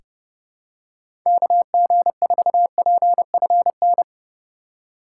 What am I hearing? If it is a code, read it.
Morse code, heard as a keyed tone, about 32 wpm, 700 Hz. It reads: KG4PFN
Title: USA Call Signs - 6 Characters - 2x Character Spacing - 30wpm